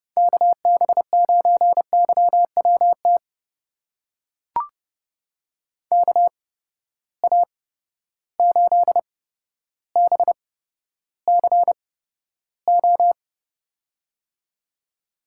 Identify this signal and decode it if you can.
Morse code, heard as a keyed tone, about 30 wpm, spaced as for 13 wpm, 700 Hz. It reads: KB9YWT KA8BCO
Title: USA Call Signs - 6 Characters - 8x Character Spacing - 30wpm